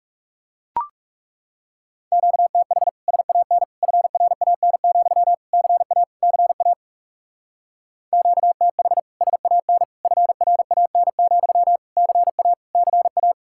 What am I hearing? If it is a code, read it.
Morse code, heard as a keyed tone, about 38 wpm, 700 Hz. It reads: QTH SAN FRAN, CA CA QTH SAN FRAN, CA CA